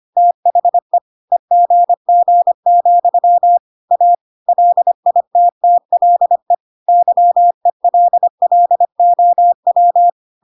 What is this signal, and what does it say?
Morse code, heard as a keyed tone, about 25 wpm, 700 Hz. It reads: THE EGG, A LITTLE YELLOW
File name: Pinocchio / Pinocchio ch5 25 25 ik1hge